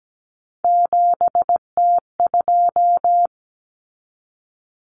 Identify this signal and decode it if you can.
Morse code, heard as a keyed tone, about 17 wpm, 690 Hz. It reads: 7T2
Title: Common Cut Numbers - 3-Digit 17wpm